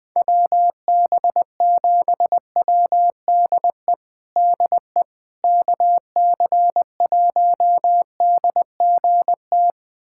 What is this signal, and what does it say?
Morse code, heard as a keyed tone, about 20 wpm, 700 Hz. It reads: WB7WDE DE KC1DGT
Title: USA Callsign DE Callsign - 6 Characters - 20wpm